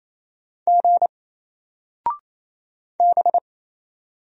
Morse code, heard as a keyed tone, about 28 wpm, 700 Hz. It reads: G B
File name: Single-Letter-Number-28wpm